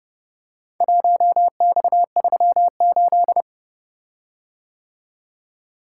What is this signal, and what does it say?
Morse code, heard as a keyed tone, about 29 wpm, 700 Hz. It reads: 1X38